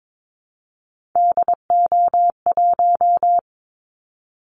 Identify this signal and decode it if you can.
Morse code, heard as a keyed tone, about 22 wpm, 700 Hz. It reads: DO1
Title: Callsign Prefixes for Top 50 DXCC - 22wpm